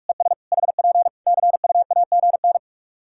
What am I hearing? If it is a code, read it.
Morse code, heard as a keyed tone, about 46 wpm, 700 Hz. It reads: ES HP CUAGN
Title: K1USN SST QSO Practice 45wpm